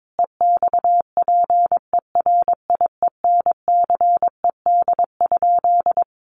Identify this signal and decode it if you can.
Morse code, heard as a keyed tone, about 22 wpm, 700 Hz. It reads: EXPERIENCED?
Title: Sentences from Top 1100 Words 22wpm